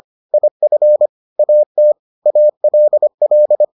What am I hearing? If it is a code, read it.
Morse code, heard as a keyed tone, about 25 wpm, 600 Hz. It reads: IF AT ALL